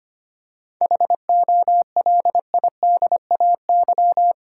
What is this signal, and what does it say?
Morse code, heard as a keyed tone, about 25 wpm, 700 Hz. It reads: HOLIDAY
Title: Seven Letter Words - 25wpm